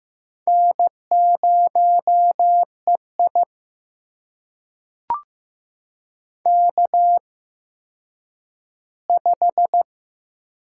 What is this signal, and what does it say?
Morse code, heard as a keyed tone, about 15 wpm, spaced as for 9 wpm, 700 Hz. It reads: N0EI K5
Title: USA Call Signs - 4 Characters - 8x Character Spacing - 15wpm